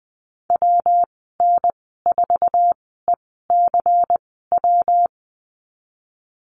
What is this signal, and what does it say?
Morse code, heard as a keyed tone, about 20 wpm, spaced as for 14 wpm, 700 Hz. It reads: WN4ECW
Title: 154 Lesson 25g - Letter p - Call Signs 20wpm